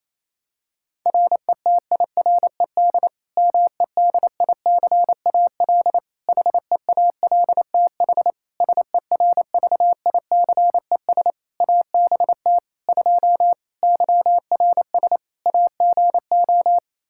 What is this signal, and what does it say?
Morse code, heard as a keyed tone, about 27 wpm, 700 Hz. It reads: RETIRED MEDICAL HEALTH SERVICES ABT 2 YRS AGO
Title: Practice QSO - Exchange 3 - More Info - 28wpm